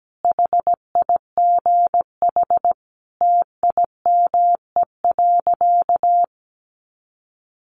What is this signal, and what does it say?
Morse code, heard as a keyed tone, about 17 wpm, 700 Hz. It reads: HIGH TIME.
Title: Sentences from Top 200 Words 17wpm